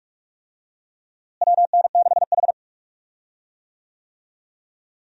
Morse code, heard as a keyed tone, about 46 wpm, 700 Hz. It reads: WN6H
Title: USA Call Signs - 4 Characters - Repeated 2x - 45wpm